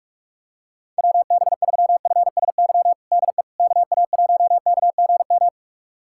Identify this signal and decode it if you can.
Morse code, heard as a keyed tone, about 46 wpm, 700 Hz. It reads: WB3FSY DE KA1KGM
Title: USA Callsign DE Callsign - 6 Characters - 45wpm